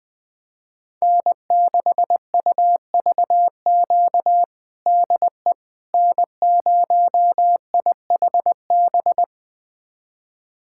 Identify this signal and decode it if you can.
Morse code, heard as a keyed tone, about 20 wpm, 700 Hz. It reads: N6UVQ DE N0IHB
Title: USA Callsign DE Callsign - 5 Characters - 20wpm